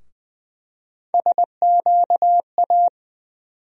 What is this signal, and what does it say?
Morse code, heard as a keyed tone, about 20 wpm, 700 Hz. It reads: SQA
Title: USA Call Sign Postfixes - 3 Letters 20wpm